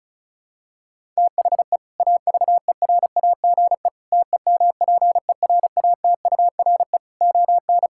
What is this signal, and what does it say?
Morse code, heard as a keyed tone, about 36 wpm, 700 Hz. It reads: THE AVERAGE TEMPERATURE ON
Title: Sentences from Top 1000 Words - Review Part 6 - 35wpm